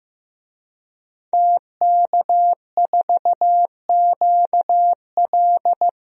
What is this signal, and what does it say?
Morse code, heard as a keyed tone, about 15 wpm, 700 Hz. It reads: TK4QL
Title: DXCC-100 Call Signs - 5 Characters - Repeated 3x - 15wpm